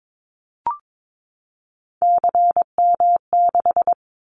Morse code, heard as a keyed tone, about 22 wpm, 700 Hz. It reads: CM6
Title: Callsign Prefixes for Top 100 DXCC - 22wpm